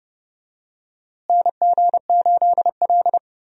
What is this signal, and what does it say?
Morse code, heard as a keyed tone, about 30 wpm, 700 Hz. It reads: NG8L
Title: USA Call Signs - 4 Characters - Repeated 3x - 30wpm